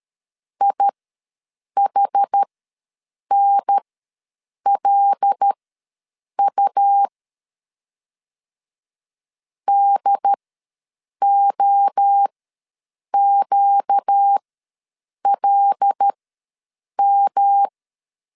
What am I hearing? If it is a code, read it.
Morse code, heard as a keyed tone, about 13 wpm, spaced as for 7 wpm, 800 Hz. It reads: IHNLU DOQLM
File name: LEZIONI / 5 LBVC / 5 L ws 12wpm 2958